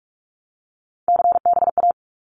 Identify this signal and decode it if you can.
Morse code, heard as a keyed tone, about 44 wpm, 700 Hz. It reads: CBA
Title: Abbreviations 45wpm